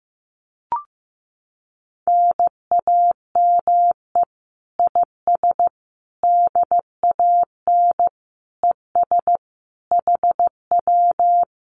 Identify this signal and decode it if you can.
Morse code, heard as a keyed tone, about 15 wpm, 700 Hz. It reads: NAME IS DAN ES HW
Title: Common QSO Phrases Encore 15wpm